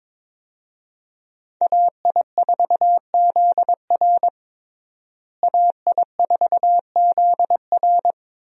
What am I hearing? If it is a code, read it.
Morse code, heard as a keyed tone, about 22 wpm, 700 Hz. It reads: AI4ZR AI4ZR
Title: USA Call Signs - 5 Characters - Repeated 3x - 22wpm